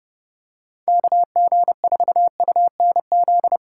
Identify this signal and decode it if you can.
Morse code, heard as a keyed tone, about 29 wpm, 700 Hz. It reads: KG4UNZ